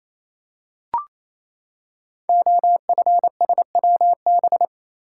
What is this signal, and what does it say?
Morse code, heard as a keyed tone, about 28 wpm, 700 Hz. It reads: OFSWB
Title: Code Groups - 5 Letters - 28wpm